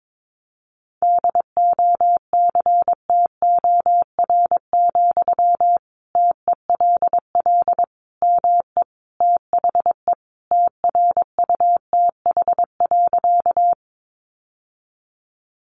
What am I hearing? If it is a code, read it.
Morse code, heard as a keyed tone, about 22 wpm, 700 Hz. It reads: DOCTOR, TELL ME THE TRUTH.
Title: Sentences from Top 1500 Words 22wpm